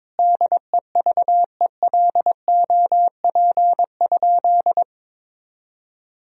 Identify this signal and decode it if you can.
Morse code, heard as a keyed tone, about 22 wpm, 700 Hz. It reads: DEVELOP?
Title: Sentences from Top 1000 Words - Review Part 2 - 22wpm